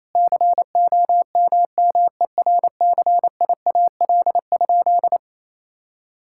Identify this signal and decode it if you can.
Morse code, heard as a keyed tone, about 28 wpm, 700 Hz. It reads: COMMERCIAL?
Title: Sentences from Top 1000 Words - Review Part 5 - 28wpm